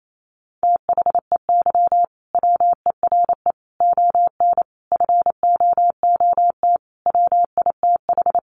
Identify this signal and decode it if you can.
Morse code, heard as a keyed tone, about 28 wpm, 700 Hz. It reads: THEY WERE ON FOOT WITH
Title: Sentences from Top 1500 Words 28wpm